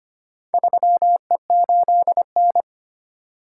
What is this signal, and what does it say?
Morse code, heard as a keyed tone, about 25 wpm, 700 Hz. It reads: 3E8N
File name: practice / 25wpm / Four Letter-Number Combinations - 25wpm